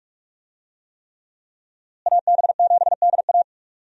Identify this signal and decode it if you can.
Morse code, heard as a keyed tone, about 46 wpm, 700 Hz. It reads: AB7DA
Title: USA Call Signs - 5 Characters - Repeated 3x - 45wpm